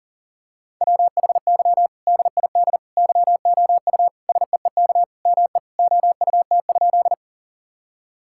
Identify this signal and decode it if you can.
Morse code, heard as a keyed tone, about 40 wpm, 700 Hz. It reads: WHY DID YOU SEEK ME OUT?
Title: Sentences from Top 1600 Words 40wpm